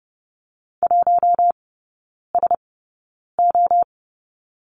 Morse code, heard as a keyed tone, about 30 wpm, spaced as for 9 wpm, 700 Hz. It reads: 1SO
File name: Lesson 15e - Number 5 - Three Characters 30wpm